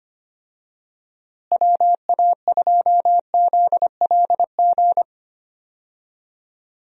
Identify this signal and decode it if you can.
Morse code, heard as a keyed tone, about 25 wpm, 700 Hz. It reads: WA2ZLG
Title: USA Call Signs - 6 Characters - Repeated 3x - 25wpm